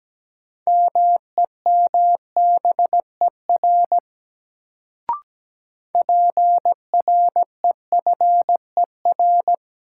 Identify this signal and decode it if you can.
Morse code, heard as a keyed tone, about 17 wpm, 700 Hz. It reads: MEMBER PREFER
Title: Sentences from Top 1100 Words - New Word Drill Alt - 17wpm